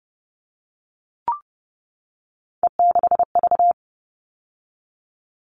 Morse code, heard as a keyed tone, about 30 wpm, 700 Hz. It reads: E6V
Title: All Cut Numbers - 3-Digit 30wpm